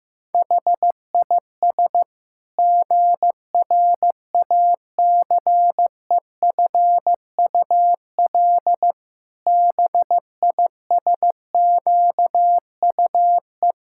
Morse code, heard as a keyed tone, about 16 wpm, 700 Hz. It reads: HIS GRACEFUL BISQUE
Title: Pangram Challenge v2 - 15wpm